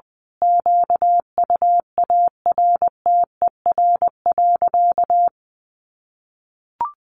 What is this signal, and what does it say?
Morse code, heard as a keyed tone, about 20 wpm, 700 Hz. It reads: QUARTER.
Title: Sentences from Top 1500 Words 20wpm